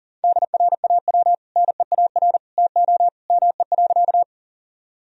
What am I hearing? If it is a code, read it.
Morse code, heard as a keyed tone, about 40 wpm, 700 Hz. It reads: DRAW NEAR TO ME.
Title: Sentences from Top 1000 Words - Review Part 1 - 40wpm